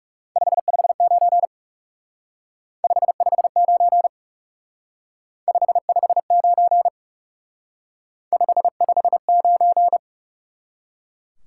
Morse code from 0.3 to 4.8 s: 559 559

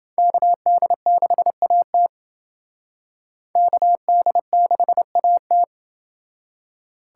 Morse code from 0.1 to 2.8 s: KD6AT

